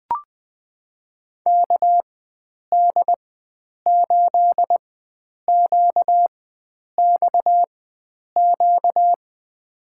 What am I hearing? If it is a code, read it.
Morse code, heard as a keyed tone, about 20 wpm, spaced as for 9 wpm, 700 Hz. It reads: KD8QXQ
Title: USA Call Signs - 4x Character Spacing - 20wpm